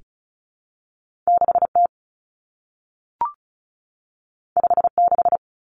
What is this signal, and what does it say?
Morse code, heard as a keyed tone, about 36 wpm, 700 Hz. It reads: 6T 56
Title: Common Cut Numbers - 2-Digit 35wpm